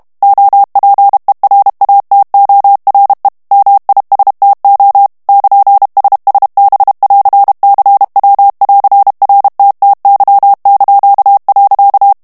Morse code, heard as a keyed tone, about 32 wpm, 800 Hz. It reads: OPERATORE MISTO <KN>SSB<AR>CW<AR>RTTY).
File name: texte clair 2 vitesse 30 mots minute